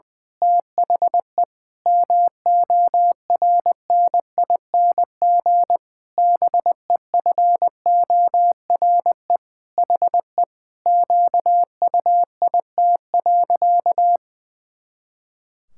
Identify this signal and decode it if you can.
Morse code, heard as a keyed tone, about 20 wpm, 700 Hz. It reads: THE MORNING BEFORE HE QUIT.
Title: Pangram Challenge v2 - 20wpm